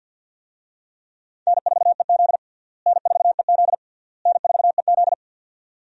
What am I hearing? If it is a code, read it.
Morse code, heard as a keyed tone, about 51 wpm, 700 Hz. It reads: N4EZ N4EZ N4EZ
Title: USA Call Signs - 4 Characters - Repeated 3x - 50wpm